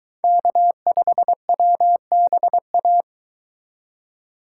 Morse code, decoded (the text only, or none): K5WBA